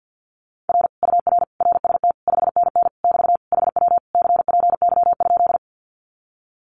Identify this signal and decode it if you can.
Morse code, heard as a keyed tone, about 48 wpm, 700 Hz. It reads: R UR RST 5NN <BT> HW CPY?
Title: Common QSO Phrases Encore 50wpm